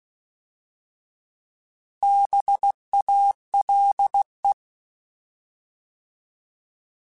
Morse code, heard as a keyed tone, about 16 wpm, 800 Hz. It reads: BALE
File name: Alphabetical / Alphabetical mp3s / Bs @15